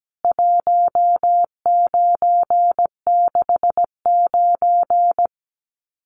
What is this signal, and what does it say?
Morse code, heard as a keyed tone, about 17 wpm, 700 Hz. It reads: 1969
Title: CWT Exchange 17wpm